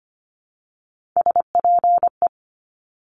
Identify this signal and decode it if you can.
Morse code, heard as a keyed tone, about 25 wpm, 700 Hz. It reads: SPE
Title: Top Three Letter Combinations in Words 25wpm